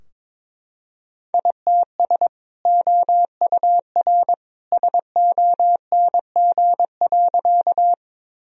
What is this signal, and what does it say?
Morse code, heard as a keyed tone, about 22 wpm, 700 Hz. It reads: ITS OUR SONG.
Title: Sentences from Top 1000 Words - Review Part 2 - 22wpm